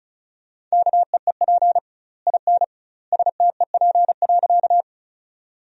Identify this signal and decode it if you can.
Morse code, heard as a keyed tone, about 36 wpm, 700 Hz. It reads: KEEP IN STEP.
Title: 2x Word-Spacing - Sentences from Top 400 Words - 35wpm